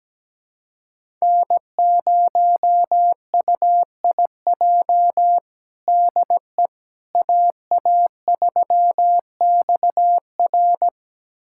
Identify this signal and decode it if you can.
Morse code, heard as a keyed tone, about 17 wpm, 700 Hz. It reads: N0UIJ DE AA3XR
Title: USA Callsign DE Callsign - 5 Characters - 17wpm